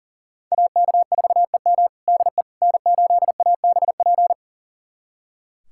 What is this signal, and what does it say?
Morse code, heard as a keyed tone, about 37 wpm, 700 Hz. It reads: AK4EM DE N8ABP
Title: USA Callsign DE Callsign - 5 Characters - 40wpm